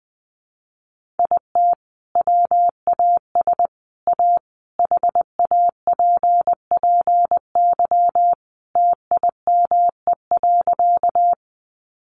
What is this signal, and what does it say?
Morse code, heard as a keyed tone, about 20 wpm, 700 Hz. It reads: IT WAS A HAPPY TIME.
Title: Sentences from Top 1000 Words 20wpm